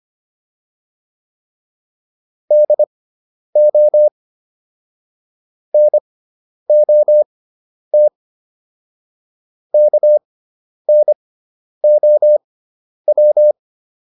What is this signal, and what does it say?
Morse code, heard as a keyed tone, about 25 wpm, spaced as for 10 wpm, 600 Hz. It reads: DO NOT KNOW